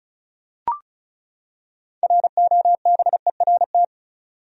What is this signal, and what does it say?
Morse code, heard as a keyed tone, about 36 wpm, 700 Hz. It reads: ROBERT
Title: USA Common Names - 35wpm